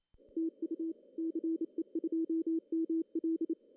Morse code, heard as a keyed tone, about 27 wpm, 330 Hz. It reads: TU CE2ML
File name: CW / cqww-cw-k5tr